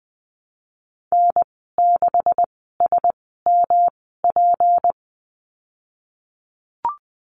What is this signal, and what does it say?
Morse code, heard as a keyed tone, about 20 wpm, spaced as for 15 wpm, 700 Hz. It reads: N6SMP
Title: 186 Lesson 30g - Letter b - Call Signs 20wpm